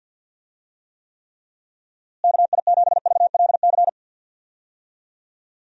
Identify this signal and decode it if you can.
Morse code, heard as a keyed tone, about 52 wpm, 700 Hz. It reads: KI7VLC
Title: USA Call Signs - 6 Characters - Repeated 3x - 50wpm